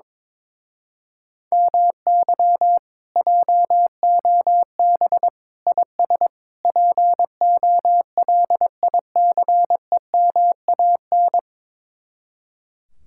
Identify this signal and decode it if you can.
Morse code, heard as a keyed tone, about 22 wpm, 700 Hz. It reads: MY JOB IS POLICEMAN
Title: Speed-Racing Common QSO Phrases - 22wpm